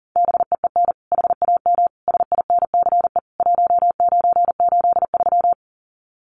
Morse code, heard as a keyed tone, about 40 wpm, 700 Hz. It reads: BEEN HAM SINCE 1983
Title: Common QSO Phrases 40wpm